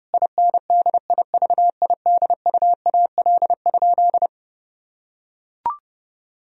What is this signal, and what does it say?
Morse code, heard as a keyed tone, about 30 wpm, 700 Hz. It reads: INDIVIDUAL?
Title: Sentences from Top 1000 Words - Review Part 4 - 30wpm